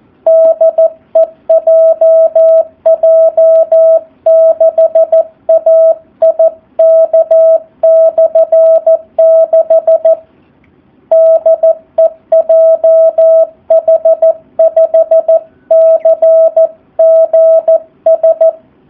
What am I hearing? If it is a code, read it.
Morse code, heard as a keyed tone, about 14 wpm, 650 Hz, with radio noise. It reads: DE JJ6AIK/6 DE JH5CGS
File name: jh8cbh / 2019cw / SP04- w